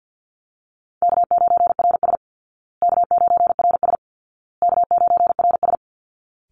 Morse code, heard as a keyed tone, about 48 wpm, 700 Hz. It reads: K9RS K9RS K9RS
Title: Call Signs Repeated 2x - 50wpm